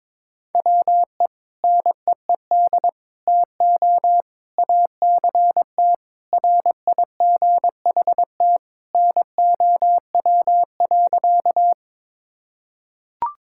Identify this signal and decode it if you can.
Morse code, heard as a keyed tone, about 22 wpm, 700 Hz. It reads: WE NEED TO ACT RIGHT NOW.
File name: practice / 22wpm / Sentences with 2-Word Phrases - 22wpm